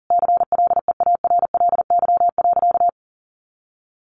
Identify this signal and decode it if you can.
Morse code, heard as a keyed tone, about 40 wpm, 700 Hz. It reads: CLEARLY.
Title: Sentences from Top 1300 Words 40wpm